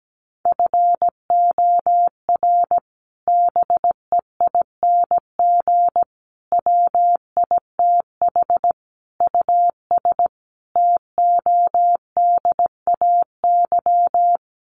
Morse code, heard as a keyed tone, about 17 wpm, 700 Hz. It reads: FOR BEING WITH US TODAY
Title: Sets of 5 Words Encore2 17wpm